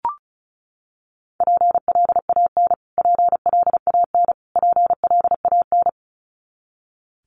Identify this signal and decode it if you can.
Morse code, heard as a keyed tone, about 34 wpm, 700 Hz. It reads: PLAN PLAN PLAN
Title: Top 500 Words Repeated 35wpm